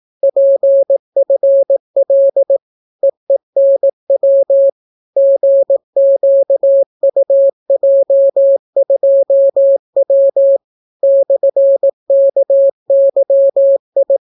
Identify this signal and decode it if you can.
Morse code, heard as a keyed tone, about 18 wpm, 550 Hz. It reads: PFL EENW GQUJ2W /KYI